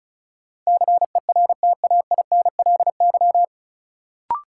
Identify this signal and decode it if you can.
Morse code, heard as a keyed tone, about 36 wpm, 700 Hz. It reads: CERTAINLY
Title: Sentences from Top 900 Words - New Word Drill - 35wpm